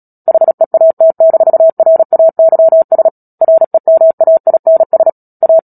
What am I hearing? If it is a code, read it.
Morse code, heard as a keyed tone, about 37 wpm, 650 Hz. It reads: HEAT-RAYS REMAINS A
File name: Book Two / 10-Chapter 10 - Epilogue - 36 WPM